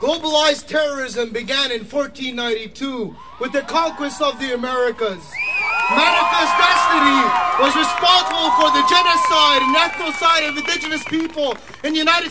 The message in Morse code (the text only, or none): none